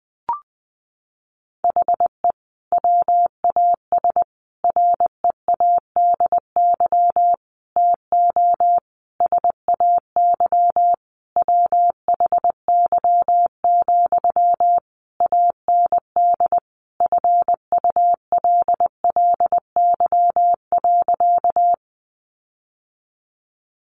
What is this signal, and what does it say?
Morse code, heard as a keyed tone, about 20 wpm, 700 Hz. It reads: HE WAS READY TO SAY WHY, AND FULLY.
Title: Sentences from Top 1300 Words 20wpm